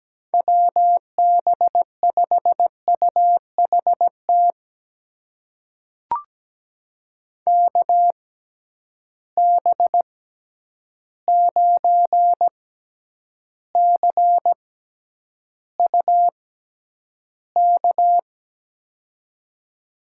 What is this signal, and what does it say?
Morse code, heard as a keyed tone, about 17 wpm, spaced as for 9 wpm, 700 Hz. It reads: WB5UHT KB9CUK
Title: USA Call Signs - 6 Characters - 6x Character Spacing - 17wpm